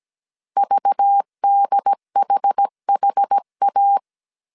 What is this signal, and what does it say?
Morse code, heard as a keyed tone, about 17 wpm, 800 Hz. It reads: VDHHA
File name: CW / DET / 1 6 1500C S C 16wpm 2733